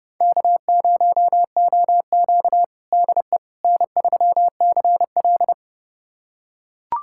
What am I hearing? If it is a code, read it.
Morse code, heard as a keyed tone, about 32 wpm, 700 Hz. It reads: K0OQ DE N3CL
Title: USA Callsign DE Callsign - 4 Characters - 30wpm